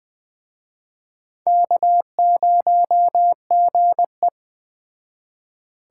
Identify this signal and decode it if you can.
Morse code, heard as a keyed tone, about 21 wpm, 700 Hz. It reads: K0GE
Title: Speed Racing - USA Call Signs 17wpm